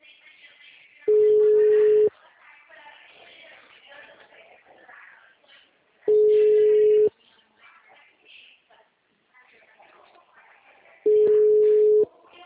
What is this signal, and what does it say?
Ringback tone (Europe, 425 Hz), with speech.
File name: out- -2018- - - .689